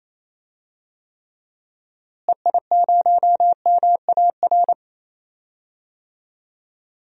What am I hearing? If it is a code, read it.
Morse code, heard as a keyed tone, about 28 wpm, 700 Hz. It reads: EI0MAR
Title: DXCC-100 Call Signs - 6 Characters - Repeated 3x - 28wpm